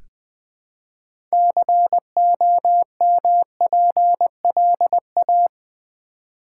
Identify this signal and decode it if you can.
Morse code, heard as a keyed tone, about 20 wpm, 700 Hz. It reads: COMPLA
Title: Build A Word - 3 Letters At A Time - 9-Letter Words - 20wpm